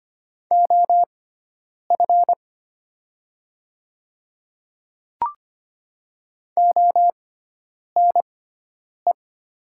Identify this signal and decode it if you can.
Morse code, heard as a keyed tone, about 25 wpm, spaced as for 9 wpm, 700 Hz. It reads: OF ONE
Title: 6x Character Spacing - Top 300 Words - 25wpm